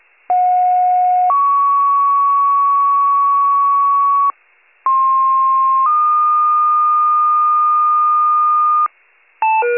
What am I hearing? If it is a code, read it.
Two-tone page, heard as a one-second tone, then a three-second tone, about 730 then 1120 Hz, about 1060 then 1220 Hz. It is heard from 0.3 to 8.9 s.